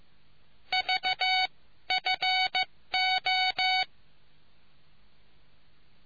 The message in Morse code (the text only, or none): VFO